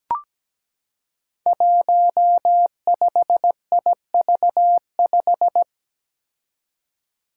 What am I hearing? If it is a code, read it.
Morse code, heard as a keyed tone, about 17 wpm, 700 Hz. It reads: 15IV5
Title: Five Challenging Letter-Number Combinations 17wpm